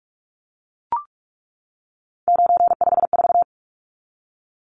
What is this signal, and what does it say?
Morse code, heard as a keyed tone, about 44 wpm, 700 Hz. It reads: 954
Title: Numbers - 3-Digit 45wpm